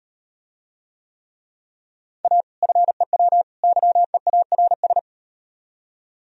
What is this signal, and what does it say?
Morse code, heard as a keyed tone, about 38 wpm, 700 Hz. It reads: A FEW YEARS